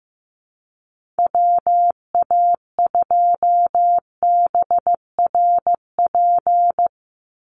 Morse code, heard as a keyed tone, about 15 wpm, 700 Hz. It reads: WA2BRP
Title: USA Call Signs 15wpm